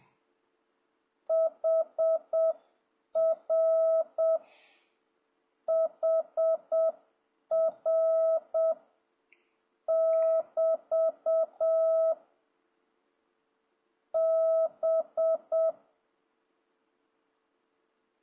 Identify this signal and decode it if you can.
Morse code, heard as a keyed tone, about 7 wpm, 650 Hz. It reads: HR HR <BT> B
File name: jh8cbh / 2026cw